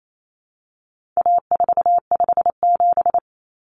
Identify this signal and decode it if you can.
Morse code, heard as a keyed tone, about 28 wpm, 700 Hz. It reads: A457
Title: Common Cut Numbers - 4-Digit 28wpm